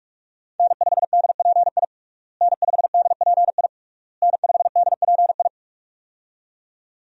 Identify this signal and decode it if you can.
Morse code, heard as a keyed tone, about 46 wpm, 700 Hz. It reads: N5DPI N5DPI N5DPI
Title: USA Call Signs - 5 Characters - Repeated 3x - 45wpm